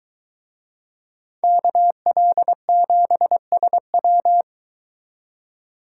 Morse code, heard as a keyed tone, about 21 wpm, 700 Hz. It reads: KL7SW